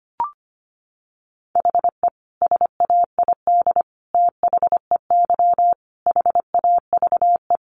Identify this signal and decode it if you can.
Morse code, heard as a keyed tone, about 25 wpm, 700 Hz. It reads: HE SAID THEY HAVE
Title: Sentences from Top 800 Words 25wpm